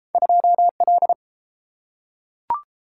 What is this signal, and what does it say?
Morse code, heard as a keyed tone, about 30 wpm, 700 Hz. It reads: 2L